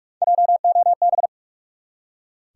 Morse code, heard as a keyed tone, about 46 wpm, 700 Hz. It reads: JOB
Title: Three Letter Words - 45wpm